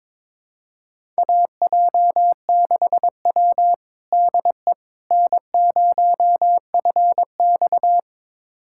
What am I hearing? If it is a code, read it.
Morse code, heard as a keyed tone, about 22 wpm, 700 Hz. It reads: AJ6W DE N0FX
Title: USA Callsign DE Callsign - 4 Characters - 22wpm